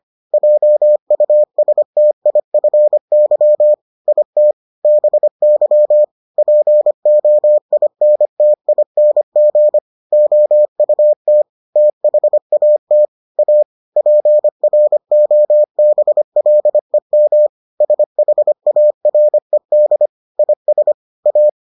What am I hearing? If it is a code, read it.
Morse code, heard as a keyed tone, about 25 wpm, 600 Hz. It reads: JUSTIFY IT BY POINTING OUT THAT A PROBLEM SHARED IS A